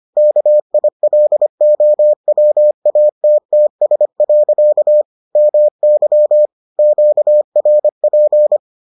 Morse code, heard as a keyed tone, about 25 wpm, 600 Hz. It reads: KILOWATTS. MY QRP